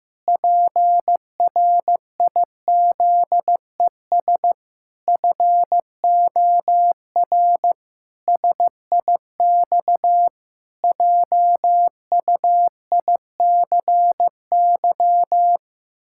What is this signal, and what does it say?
Morse code, heard as a keyed tone, about 16 wpm, 700 Hz. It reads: PRIZES FOR SIX JUICY
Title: Pangram Challenge v2 - 15wpm